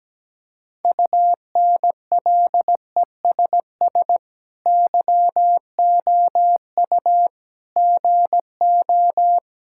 Morse code, heard as a keyed tone, about 17 wpm, 700 Hz. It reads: UNLESS YOU GO
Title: Sentences from Top 1000 Words - Review Part 6 - 17wpm